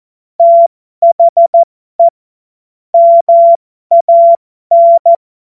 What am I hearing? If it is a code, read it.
Morse code, heard as a keyed tone, about 14 wpm, spaced as for 12 wpm, 680 Hz. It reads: THE MAN
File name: Morsekurs / Witze